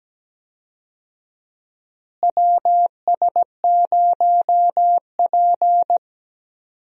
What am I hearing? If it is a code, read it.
Morse code, heard as a keyed tone, about 17 wpm, 700 Hz. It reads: WS0P
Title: USA Call Signs - 4 Characters - Repeated 3x - 17wpm